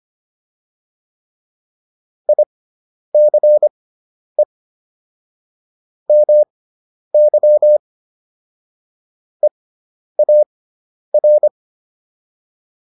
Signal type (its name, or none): Morse code